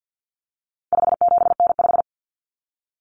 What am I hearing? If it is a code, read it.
Morse code, heard as a keyed tone, about 50 wpm, 700 Hz. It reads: 57N5
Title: Common Cut Numbers - 4-Digit 50wpm